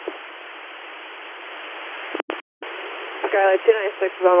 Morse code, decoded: none

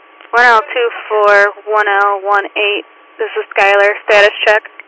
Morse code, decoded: none

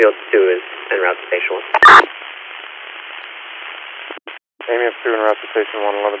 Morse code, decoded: none

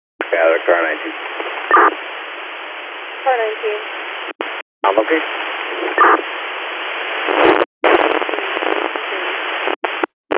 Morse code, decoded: none

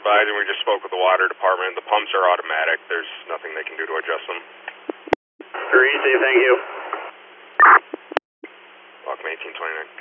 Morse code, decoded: none

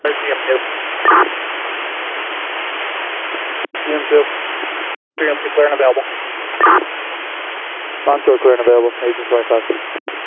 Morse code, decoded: none

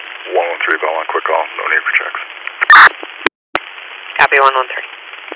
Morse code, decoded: none

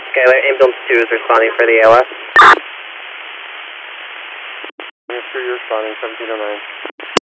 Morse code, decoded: none

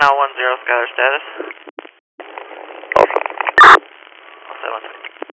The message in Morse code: none

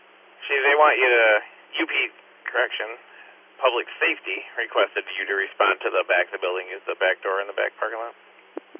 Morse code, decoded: none

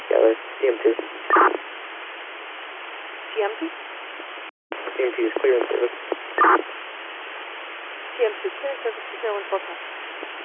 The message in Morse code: none

